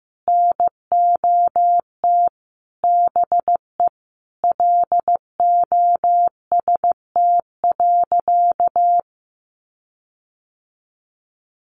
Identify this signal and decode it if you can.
Morse code, heard as a keyed tone, about 15 wpm, 700 Hz. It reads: NOT BE LOST.
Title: Sentences from Top 1400 Words 15wpm